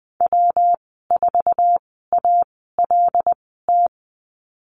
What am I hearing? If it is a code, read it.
Morse code, heard as a keyed tone, about 20 wpm, spaced as for 15 wpm, 700 Hz. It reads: W4ALT
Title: 186 Lesson 30g - Letter b - Call Signs 20wpm